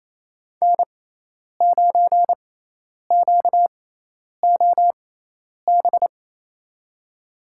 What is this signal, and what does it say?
Morse code, heard as a keyed tone, about 28 wpm, spaced as for 10 wpm, 700 Hz. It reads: N9QOB
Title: USA Call Signs - 5 Characters - 6x Character Spacing - 28wpm